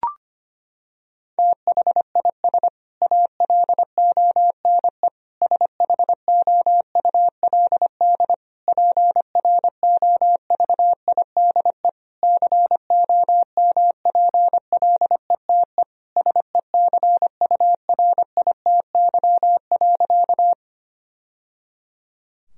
Morse code, heard as a keyed tone, about 25 wpm, 700 Hz. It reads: THIS ALONE SHOULD PROVIDE COMPLETE SECURITY.
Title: Sentences from Top 1000 Words - Review Part 4 - 25wpm